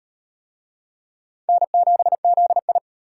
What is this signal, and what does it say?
Morse code, heard as a keyed tone, about 38 wpm, 700 Hz. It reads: N7ZI